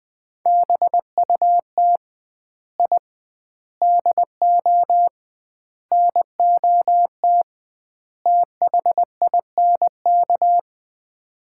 Morse code, heard as a keyed tone, about 20 wpm, 700 Hz. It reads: BUT I DO NOT THINK
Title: 2x Word-Spacing - Sets of 5 Words - 20wpm